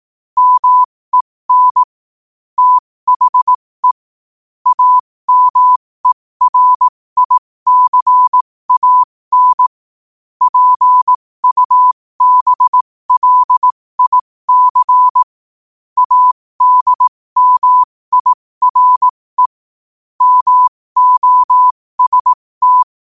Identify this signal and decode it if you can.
Morse code, heard as a keyed tone, about 18 wpm, spaced as for 15 wpm, 1000 Hz. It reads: MEN THE AMERICAN PUBLIC ADMIRE MOST
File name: qotdfiles / 15wpm